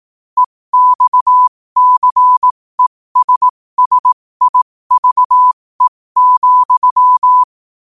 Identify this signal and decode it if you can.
Morse code, heard as a keyed tone, about 18 wpm, spaced as for 15 wpm, 1000 Hz. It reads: EXCESSIVE,
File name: qotdfiles / 15wpm